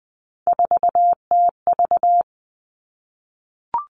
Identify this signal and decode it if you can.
Morse code, heard as a keyed tone, about 20 wpm, 700 Hz. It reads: 4TV
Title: Three Challenging Letter-Number Combinations 20wpm